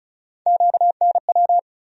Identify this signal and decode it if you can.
Morse code, heard as a keyed tone, about 36 wpm, 700 Hz. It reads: QNW
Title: USA Call Sign Postfixes - 3 Letters 35wpm